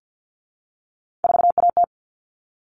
Morse code, heard as a keyed tone, about 50 wpm, 700 Hz. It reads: 4AT
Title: All Cut Numbers - 3-Digit 50wpm